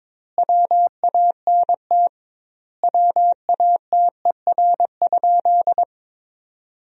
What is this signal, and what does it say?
Morse code, heard as a keyed tone, about 22 wpm, 700 Hz. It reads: WANT WATER?
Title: 2x Word-Spacing - Sentences from Top 200 Words - 22wpm